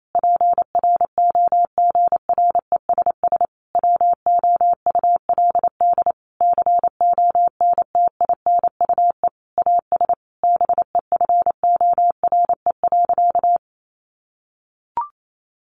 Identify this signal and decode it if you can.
Morse code, heard as a keyed tone, about 28 wpm, 700 Hz. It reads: PROGRESS WOULD CONTINUE AS BEFORE.
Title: Sentences from Top 1400 Words 28wpm